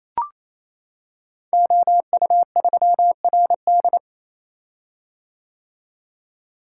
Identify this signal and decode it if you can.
Morse code, heard as a keyed tone, about 28 wpm, 700 Hz. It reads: OU3RD
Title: Code Groups - 5 Letter:Numbers - 28wpm